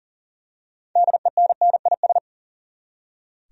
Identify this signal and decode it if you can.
Morse code, heard as a keyed tone, about 40 wpm, 700 Hz. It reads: DENNIS
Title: USA Common Names - 40wpm